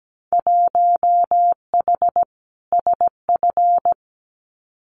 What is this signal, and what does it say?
Morse code, heard as a keyed tone, about 17 wpm, 700 Hz. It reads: 1H SF
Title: Winter Field Day Exchanges 17wpm